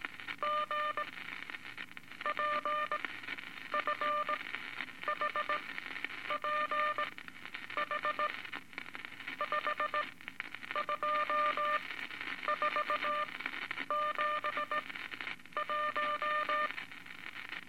Morse code, heard as a keyed tone, about 19 wpm, spaced as for 8 wpm, 1200 Hz, with noise. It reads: GPFHP H5247 1
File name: Morse Code Records / 17A LCM 162 Eight Words Per Minute